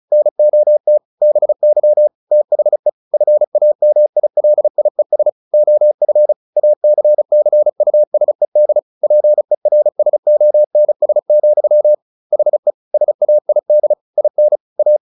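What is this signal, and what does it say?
Morse code, heard as a keyed tone, about 34 wpm, 600 Hz. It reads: NOT BY THE FAMILIES OF ACCUSED PERSONS, HE SAID IN A
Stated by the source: Morse practice at 35 wpm from CW med Gnister, sending Ken Follett's novel Never in English.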